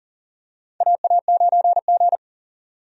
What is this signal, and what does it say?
Morse code, heard as a keyed tone, about 40 wpm, 700 Hz. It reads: AA9G
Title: USA Call Signs - 4 Characters - 40wpm